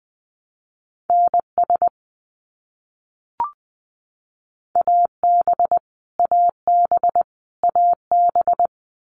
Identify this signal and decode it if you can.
Morse code, heard as a keyed tone, about 20 wpm, 700 Hz. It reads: NS AB AB AB
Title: Canadian Provinces and Territories - Repeated 3x - 20wpm